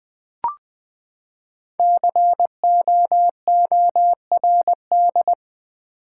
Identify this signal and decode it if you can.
Morse code, heard as a keyed tone, about 20 wpm, 700 Hz. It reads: COORD
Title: Build A Word - 5 Letters At A Time - 12-Letter Words - 20wpm